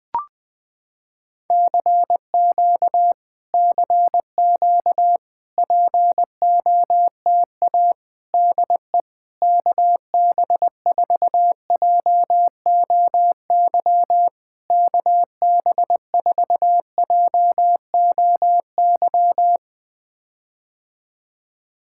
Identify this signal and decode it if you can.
Morse code, heard as a keyed tone, about 21 wpm, 700 Hz. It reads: CQ CQ POTA DE KB4JOY KB4JOY
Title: POTA QSO Practice - 20wpm